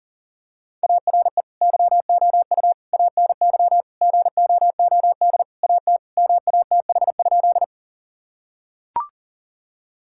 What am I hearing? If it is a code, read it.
Morse code, heard as a keyed tone, about 37 wpm, 700 Hz. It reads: ARE YOU ANY GOOD AT MATH?
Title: Sentences from Top 1000 Words - Review Part 6 - 40wpm